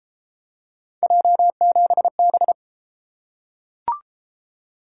Morse code, heard as a keyed tone, about 29 wpm, 700 Hz. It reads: J7B